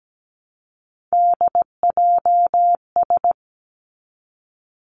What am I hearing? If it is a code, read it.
Morse code, heard as a keyed tone, about 17 wpm, 700 Hz. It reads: DJS
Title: Three Challenging Letter-Number Combinations 17wpm